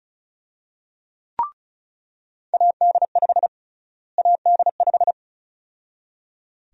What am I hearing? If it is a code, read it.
Morse code, heard as a keyed tone, about 36 wpm, 700 Hz. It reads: AD5 AD5
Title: USA Call Sign Prefixes - Repeated 2x - 35wpm